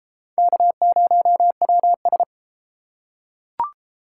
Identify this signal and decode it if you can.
Morse code, heard as a keyed tone, about 30 wpm, 700 Hz. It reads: K0WS